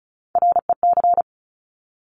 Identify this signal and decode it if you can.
Morse code, heard as a keyed tone, about 36 wpm, 700 Hz. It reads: REC
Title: Top Three Letter Combinations in Words 35wpm